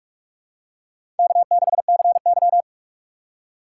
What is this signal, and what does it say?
Morse code, heard as a keyed tone, about 48 wpm, 700 Hz. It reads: K6CY
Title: USA Call Signs - 4 Characters - 6x Character Spacing - 45wpm